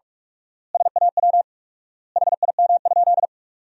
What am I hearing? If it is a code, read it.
Morse code, heard as a keyed tone, about 42 wpm, 700 Hz. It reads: SAW HIM?
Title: 4x Word-Spacing - Sentences from Top 300 Words - 45wpm